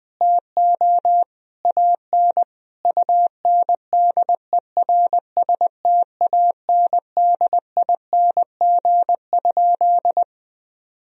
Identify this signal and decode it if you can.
Morse code, heard as a keyed tone, about 20 wpm, 700 Hz. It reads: TO AN UNDERSTANDING?
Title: Sentences from Top 1000 Words - Review Part 3 - 20wpm